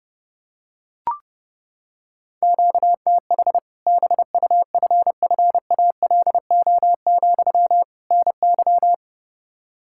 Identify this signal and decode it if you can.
Morse code, heard as a keyed tone, about 30 wpm, 700 Hz. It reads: QTH BUFFALO, NY
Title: Practice QSO - Exchange 1 - Beginning - 30wpm